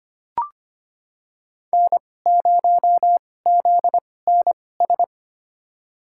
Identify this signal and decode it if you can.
Morse code, heard as a keyed tone, about 25 wpm, spaced as for 18 wpm, 700 Hz. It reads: N0ZNS
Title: USA Call Signs - 5 Characters - 2x Character Spacing - 25wpm